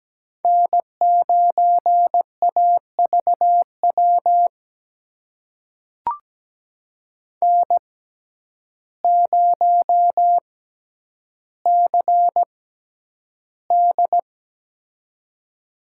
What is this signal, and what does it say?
Morse code, heard as a keyed tone, about 17 wpm, spaced as for 9 wpm, 700 Hz. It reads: N9AVW N0CD
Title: USA Call Signs - 6x Character Spacing - 17wpm